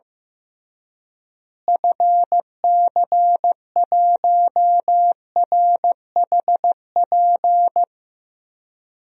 Morse code, heard as a keyed tone, about 15 wpm, 700 Hz. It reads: FC1RHP
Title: DXCC-100 Call Signs - 6 Characters - Repeated 3x - 15wpm